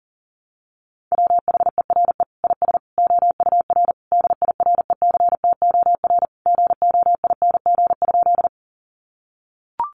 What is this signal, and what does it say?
Morse code, heard as a keyed tone, about 40 wpm, 700 Hz. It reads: WHERE IS OUR DIRECTOR GOING?
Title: Sentences from Top 1300 Words 40wpm